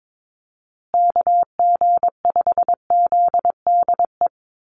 Morse code, heard as a keyed tone, about 22 wpm, 700 Hz. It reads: KG5ZDE
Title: USA Call Signs Repeated 3x 22wpm